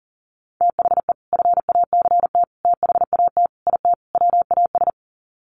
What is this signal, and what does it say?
Morse code, heard as a keyed tone, about 40 wpm, 700 Hz. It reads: THE FACT THAT IT WAS
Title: Sets of 5 Words Encore2 40wpm